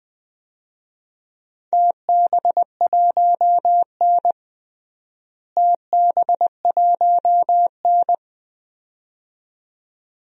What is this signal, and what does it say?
Morse code, heard as a keyed tone, about 20 wpm, 700 Hz. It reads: TB1N TB1N
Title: DXCC-100 Call Signs - 4 Characters - Repeated 2x - 20wpm